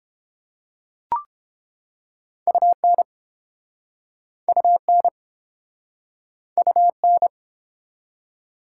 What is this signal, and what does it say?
Morse code, heard as a keyed tone, about 30 wpm, 700 Hz. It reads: UN UN UN